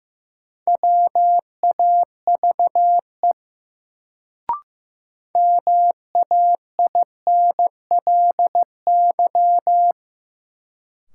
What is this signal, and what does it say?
Morse code, heard as a keyed tone, about 15 wpm, 700 Hz. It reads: WAVE MAINLY
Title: Sentences from Top 1400 Words - New Word Drill - 15wpm